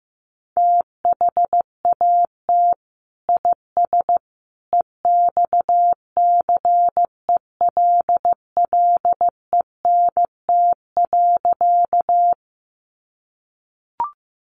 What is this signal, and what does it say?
Morse code, heard as a keyed tone, about 15 wpm, 700 Hz. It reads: THAT IS EXCELLENT.
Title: Sentences from Top 1000 Words 15wpm